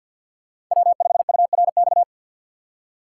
Morse code, heard as a keyed tone, about 53 wpm, 700 Hz. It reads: W5URX
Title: USA Call Signs - 5 Characters - 8x Character Spacing - 50wpm